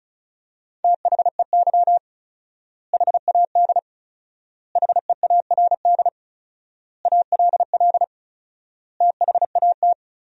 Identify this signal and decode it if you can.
Morse code, heard as a keyed tone, about 36 wpm, 700 Hz. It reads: THEY HAD HEARD ALL THAT
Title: 4x Word-Spacing - Sentences from Top 400 Words - 35wpm